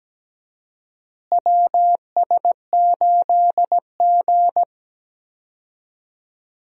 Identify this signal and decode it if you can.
Morse code, heard as a keyed tone, about 17 wpm, 700 Hz. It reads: WS8G
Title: USA Call Signs - 4 Characters - Repeated 3x - 17wpm